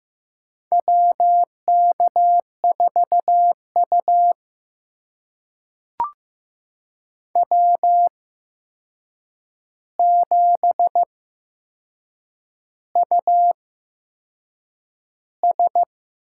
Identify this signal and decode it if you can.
Morse code, heard as a keyed tone, about 15 wpm, spaced as for 6 wpm, 700 Hz. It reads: WK4U W7US
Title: USA Call Signs - 4 Characters - 8x Character Spacing - 15wpm